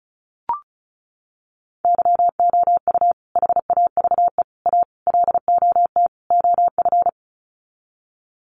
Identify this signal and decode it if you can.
Morse code, heard as a keyed tone, about 34 wpm, 700 Hz. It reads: YOU HAVE A LOT OF
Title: Sets of 5 Words Encore 35wpm